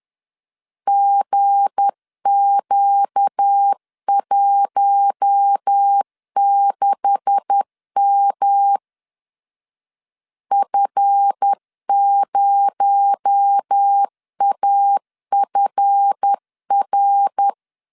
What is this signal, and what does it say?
Morse code, heard as a keyed tone, about 10 wpm, 800 Hz. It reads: GQ16M F0AFR
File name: CW / DET / 1 9 1500C S C 10wpm 4557